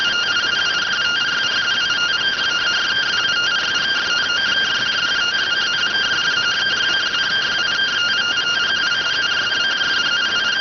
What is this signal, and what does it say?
Two-tone teleprinter signal, about 170 Hz shift.